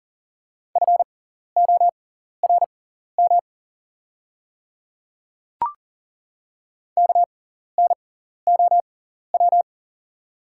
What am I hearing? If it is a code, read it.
Morse code, heard as a keyed tone, about 40 wpm, spaced as for 14 wpm, 690 Hz. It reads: FORM KNOW
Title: 6x Character Spacing - Top 200 Words - 40wpm